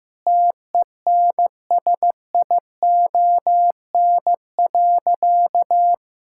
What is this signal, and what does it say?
Morse code, heard as a keyed tone, about 15 wpm, 700 Hz. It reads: TENSION.
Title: Sentences from Top 1600 Words 15wpm